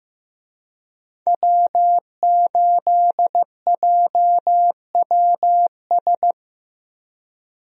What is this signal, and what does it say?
Morse code, heard as a keyed tone, about 15 wpm, 700 Hz. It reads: W8JWS
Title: USA Call Signs - 5 Characters - Repeated 2x - 15wpm